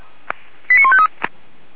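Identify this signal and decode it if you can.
Five-tone selective call (ZVEI), heard as quick runs of five tones, tones of about 70 ms. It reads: 89153
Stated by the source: ZVEI five-tone selective call received on an SDR.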